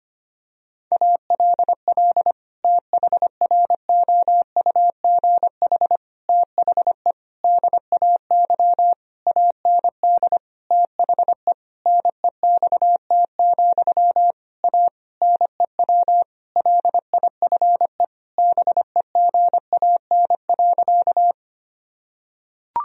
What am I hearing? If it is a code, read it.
Morse code, heard as a keyed tone, about 25 wpm, 700 Hz. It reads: ALL THROUGH THE DAY AND THE NEXT, A NEW LIFE BEGAN.
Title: Sentences from Top 1000 Words - Review Part 1 - 25wpm